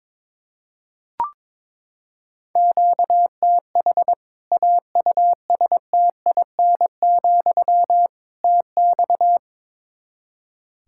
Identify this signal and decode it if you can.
Morse code, heard as a keyed tone, about 22 wpm, 700 Hz. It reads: QTH AUSTIN, TX
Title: Practice QSO - Exchange 1 - Beginning - 22wpm